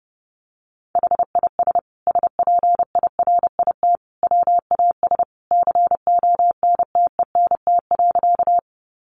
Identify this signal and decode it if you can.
Morse code, heard as a keyed tone, about 30 wpm, 700 Hz. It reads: HIS SPIRIT WAS CONTENT.
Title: Sentences from Top 1400 Words 30wpm